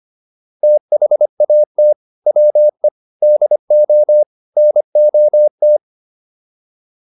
Morse code, heard as a keyed tone, about 25 wpm, 600 Hz. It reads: THAT WE DO NOT